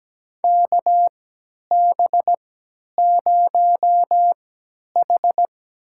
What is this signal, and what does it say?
Morse code, heard as a keyed tone, about 17 wpm, spaced as for 10 wpm, 700 Hz. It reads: KB0H
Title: USA Call Signs - 4 Characters - 3x Character Spacing - 17wpm